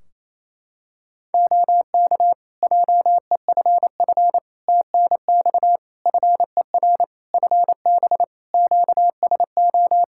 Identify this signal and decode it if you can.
Morse code, heard as a keyed tone, about 28 wpm, 700 Hz. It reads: OK JEFF TNX FER FB QSO
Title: Practice QSO - Exchange 4 - Ending - 28wpm